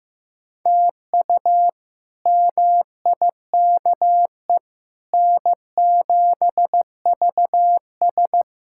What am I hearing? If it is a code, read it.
Morse code, heard as a keyed tone, about 15 wpm, 700 Hz. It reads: TU MIKE N7VS
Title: K1USN SST QSO Practice - 15wpm